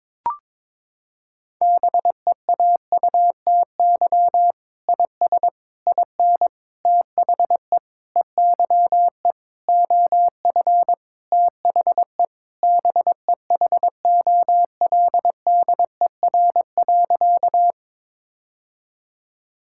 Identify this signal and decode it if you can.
Morse code, heard as a keyed tone, about 22 wpm, 700 Hz. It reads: BEAUTY IS IN THE EYE OF THE BEHOLDER.
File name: Sentences with 7-Word Phrases - 22wpm